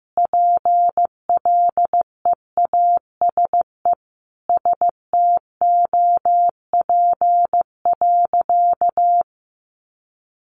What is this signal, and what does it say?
Morse code, heard as a keyed tone, about 15 wpm, 700 Hz. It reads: PLEASE STOP.
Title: Sentences from Top 1500 Words 15wpm